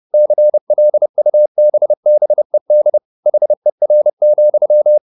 Morse code, heard as a keyed tone, about 30 wpm, 600 Hz. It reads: CLUBBED HER,
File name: the old man and the sea / the old man and the sea 1 30wpm